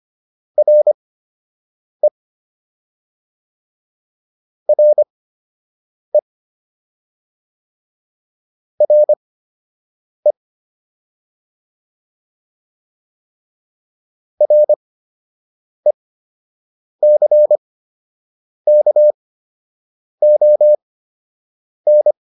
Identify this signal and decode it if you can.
Morse code, heard as a keyed tone, about 25 wpm, spaced as for 7 wpm, 600 Hz. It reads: RE RE RE RECKON